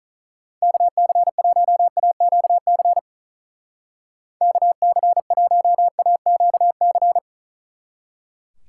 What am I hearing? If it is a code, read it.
Morse code, heard as a keyed tone, about 45 wpm, 700 Hz. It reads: KC1AQC KC1AQC